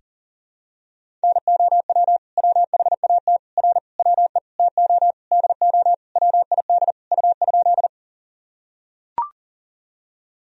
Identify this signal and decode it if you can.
Morse code, heard as a keyed tone, about 40 wpm, 700 Hz. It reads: NOW WHAT R WE TO DO WID U?
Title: Sentences from Top 100 Words and Abbreviations 40wpm